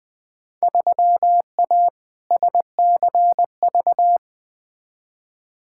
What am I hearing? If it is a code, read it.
Morse code, heard as a keyed tone, about 20 wpm, 700 Hz. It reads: 3A SCV
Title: Field Day Exchanges - 20wpm